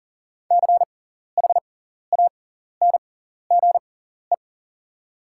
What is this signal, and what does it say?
Morse code, heard as a keyed tone, about 40 wpm, spaced as for 14 wpm, 690 Hz. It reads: CHANGE
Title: 6x Character Spacing - Top 200 Words - 40wpm